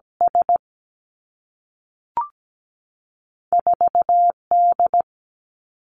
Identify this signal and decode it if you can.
Morse code, heard as a keyed tone, about 17 wpm, spaced as for 3 wpm, 700 Hz. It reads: S4 D
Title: Two Challenging Letter-Number Combinations 17wpm